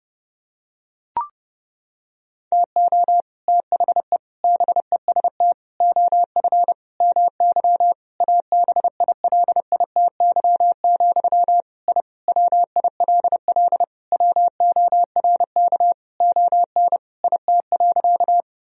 Morse code, heard as a keyed tone, about 30 wpm, 700 Hz. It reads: TO THE BEST OF MY ABILITY, I WILL WORK ON IT.
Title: Sentences from Top 1000 Words - Review Part 3 - 30wpm